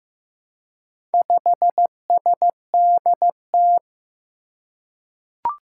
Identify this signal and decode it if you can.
Morse code, heard as a keyed tone, about 15 wpm, 700 Hz. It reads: 5SDT
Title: Four Challenging Letter-Number Combinations 15wpm